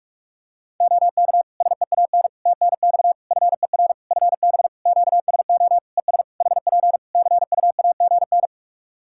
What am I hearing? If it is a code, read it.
Morse code, heard as a keyed tone, about 46 wpm, 700 Hz. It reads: OK SEAN TNX FER FB QSO ES HP CUAGN
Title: Practice QSO - Exchange 4 - Ending - 45wpm